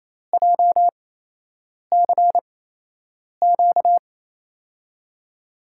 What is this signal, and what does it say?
Morse code, heard as a keyed tone, about 28 wpm, spaced as for 8 wpm, 700 Hz. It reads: JCQ
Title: USA Call Signs - 6 Characters - 8x Character Spacing - 28wpm